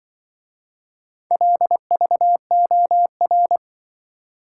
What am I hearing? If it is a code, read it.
Morse code, heard as a keyed tone, about 25 wpm, 700 Hz. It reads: LVOR
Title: Speed-Racing - Four Letter-Number Combinations - 20wpm